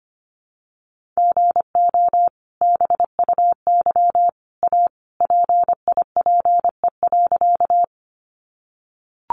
Morse code, heard as a keyed tone, about 25 wpm, 700 Hz. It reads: GO BUY A PIPE.
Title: Sentences from Top 1500 Words 25wpm